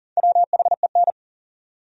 Morse code, heard as a keyed tone, about 54 wpm, spaced as for 47 wpm, 700 Hz. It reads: WHEN
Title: Speed-Racing Top 100 Words 40wpm